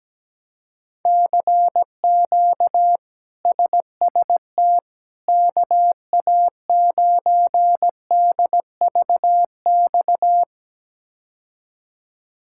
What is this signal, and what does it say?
Morse code, heard as a keyed tone, about 17 wpm, 700 Hz. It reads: CQ SST KA9DVX
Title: K1USN SST QSO Practice - 17wpm